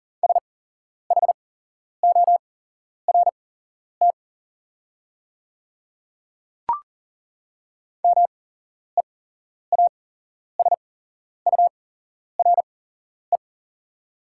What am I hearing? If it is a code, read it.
Morse code, heard as a keyed tone, about 41 wpm, spaced as for 11 wpm, 690 Hz. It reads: SHORT MEASURE
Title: 8x Character Spacing - Top 500 Words - 40wpm